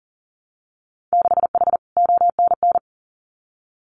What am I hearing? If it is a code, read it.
Morse code, heard as a keyed tone, about 39 wpm, 700 Hz. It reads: 6H ONN
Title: Winter Field Day Exchanges Encore 40wpm